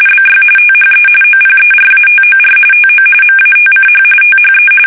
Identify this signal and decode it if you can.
RTTY, about 45.45 baud. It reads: BY THE OPERATOR TO MATCH THE
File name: fsk850 / teletype-corp